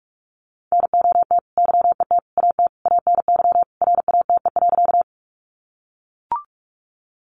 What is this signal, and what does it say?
Morse code, heard as a keyed tone, about 44 wpm, 700 Hz. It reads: NOT YET AT ANY RATE.
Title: Sentences from Top 700 Words 45wpm